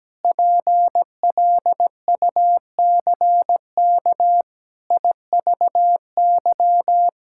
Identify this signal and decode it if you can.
Morse code, heard as a keyed tone, about 17 wpm, 700 Hz. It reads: PLUCK IVY
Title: Pangram Challenge v2 - 17wpm